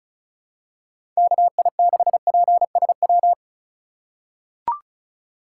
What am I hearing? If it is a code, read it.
Morse code, heard as a keyed tone, about 36 wpm, 700 Hz. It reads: KI6PSW
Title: USA Call Signs - 6 Characters - 4x Character Spacing - 35wpm